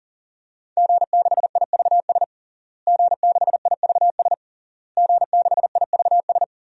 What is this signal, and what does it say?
Morse code, heard as a keyed tone, about 40 wpm, 700 Hz. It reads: G6IVS G6IVS G6IVS
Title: DXCC-100 Call Signs - 5 Characters - Repeated 3x - 40wpm